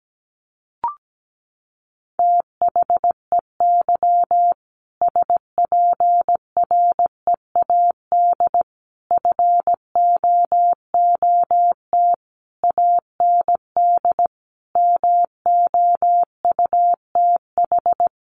Morse code, heard as a keyed tone, about 17 wpm, 700 Hz. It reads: THEY SPREAD FOOT AND MOUTH
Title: Sentences from Top 1500 Words 17wpm